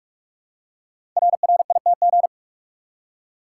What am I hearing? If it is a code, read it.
Morse code, heard as a keyed tone, about 44 wpm, 700 Hz. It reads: RRITG
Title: Code Groups - 5 Letter:Numbers - 45wpm